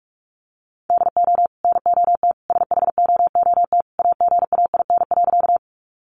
Morse code, heard as a keyed tone, about 44 wpm, 700 Hz. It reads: DO NOT SHOOT AGAIN.
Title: Sentences from Top 1200 Words 45wpm